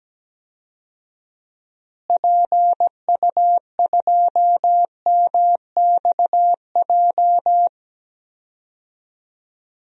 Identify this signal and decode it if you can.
Morse code, heard as a keyed tone, about 17 wpm, 700 Hz. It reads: PU2MXJ
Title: DXCC-100 Call Signs - 6 Characters - Repeated 2x - 17wpm